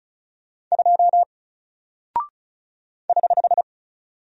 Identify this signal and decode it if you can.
Morse code, heard as a keyed tone, about 36 wpm, 700 Hz. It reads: 2 <HH>
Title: Single Character (Letter:Number:Punctuation:Prosign) 35wpm